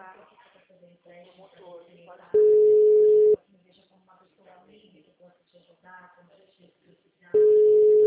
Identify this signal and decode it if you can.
Ringback tone (Europe, 425 Hz), with speech.
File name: out- -2028- - - .7584